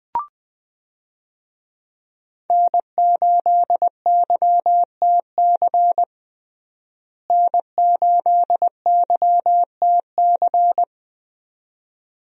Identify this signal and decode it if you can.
Morse code, heard as a keyed tone, about 20 wpm, 700 Hz. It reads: N8YTC N8YTC
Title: USA Call Signs - 5 Characters - Repeated 2x - 20wpm